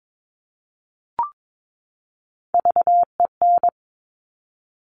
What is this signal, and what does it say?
Morse code, heard as a keyed tone, about 22 wpm, 700 Hz. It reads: VEN
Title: Top Three Letter Combinations in Words 22wpm